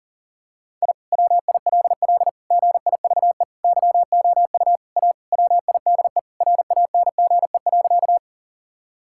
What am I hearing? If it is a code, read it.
Morse code, heard as a keyed tone, about 40 wpm, 700 Hz. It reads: I WILL GIVE YOU A WIDE RANGE.
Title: Sentences from Top 1000 Words - Review Part 6 - 40wpm